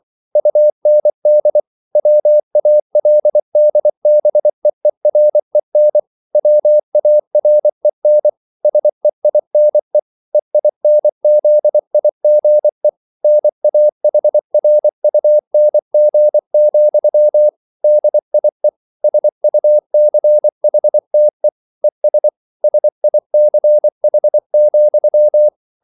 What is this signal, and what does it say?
Morse code, heard as a keyed tone, about 24 wpm, 600 Hz. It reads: UND WALDBEEREN WAREN SEINE EINZIGE NAHRUNG, DIE SUCHTE ES SICH,